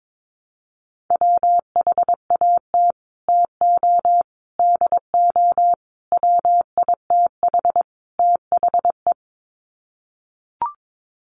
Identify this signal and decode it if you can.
Morse code, heard as a keyed tone, about 22 wpm, 700 Hz. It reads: WHAT TO DO WITH THE
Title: Sets of 5 Words Encore2 22wpm